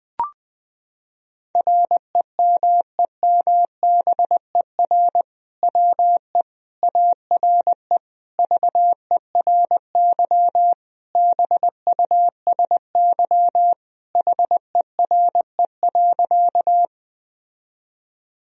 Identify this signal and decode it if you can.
Morse code, heard as a keyed tone, about 20 wpm, 700 Hz. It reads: REMEMBER WE ARE VERY BUSY HERE.
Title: Sentences from Top 1000 Words - Review Part 3 - 20wpm